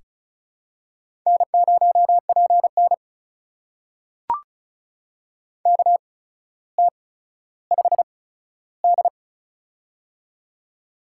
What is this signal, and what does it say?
Morse code, heard as a keyed tone, about 36 wpm, spaced as for 15 wpm, 700 Hz. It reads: N0PN KT5D
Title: USA Call Signs - 4 Characters - 8x Character Spacing - 35wpm